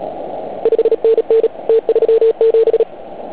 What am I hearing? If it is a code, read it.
Morse code, heard as a keyed tone, about 37 wpm, 450 Hz. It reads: 5NN T37
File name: Archiv / CUC149 / OK1IF CUC149